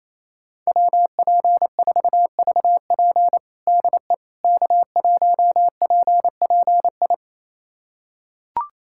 Morse code, heard as a keyed tone, about 28 wpm, 700 Hz. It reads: WP4VP DE K1PPI
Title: USA Callsign DE Callsign - 5 Characters - 28wpm